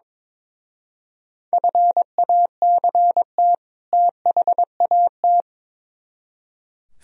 Morse code, heard as a keyed tone, about 24 wpm, 700 Hz. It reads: FACT THAT
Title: Speed-Racing - Sets of 2 Words 22wpm